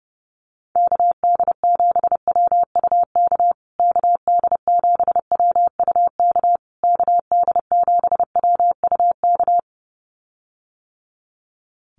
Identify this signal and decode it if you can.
Morse code, heard as a keyed tone, about 30 wpm, 700 Hz. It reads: KD7WUK KD7WUK KD7WUK
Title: USA Call Signs Repeated 3x 30wpm